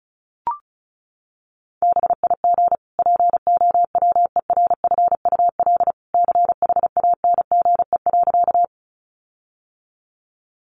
Morse code, heard as a keyed tone, about 34 wpm, 700 Hz. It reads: BIG POWERFUL CHANGE.
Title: Sentences from Top 1300 Words 35wpm